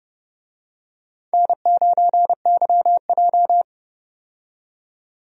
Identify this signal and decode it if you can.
Morse code, heard as a keyed tone, about 31 wpm, 700 Hz. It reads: N9YJ
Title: USA Call Signs - 4 Characters - 30wpm